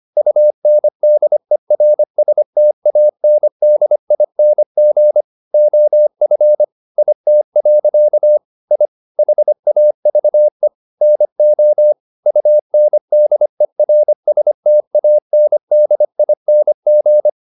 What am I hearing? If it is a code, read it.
Morse code, heard as a keyed tone, about 25 wpm, 600 Hz. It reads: UNDERSTANDING OF IT. I HAVE NO UNDERSTANDING